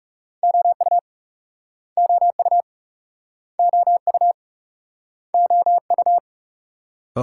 Morse code, heard as a keyed tone, about 38 wpm, 700 Hz. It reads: OU OU OU OU